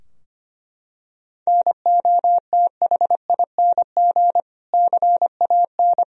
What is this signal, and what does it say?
Morse code, heard as a keyed tone, about 25 wpm, 700 Hz. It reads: NOTHING CAN
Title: Sentences from Top 1000 Words - Review Part 2 - 25wpm